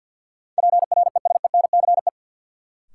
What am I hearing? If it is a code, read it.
Morse code, heard as a keyed tone, about 53 wpm, 700 Hz. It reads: PRESENCE
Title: Sets of 3 Words - New Word Drill - 50wpm